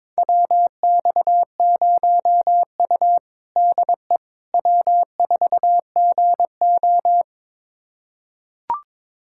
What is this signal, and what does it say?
Morse code, heard as a keyed tone, about 22 wpm, 700 Hz. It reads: WX0U DE W4GO
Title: USA Callsign DE Callsign - 4 Characters - 22wpm